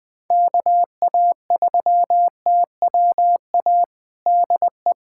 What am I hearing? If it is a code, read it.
Morse code, heard as a keyed tone, about 21 wpm, 700 Hz. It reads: KA3TWA DE
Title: USA Callsign DE Callsign - 6 Characters - 20wpm